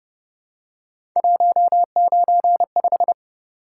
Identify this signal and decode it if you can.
Morse code, heard as a keyed tone, about 29 wpm, 700 Hz. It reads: 195
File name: Speed-Racing - Three Letter-Number Combinations - 25wpm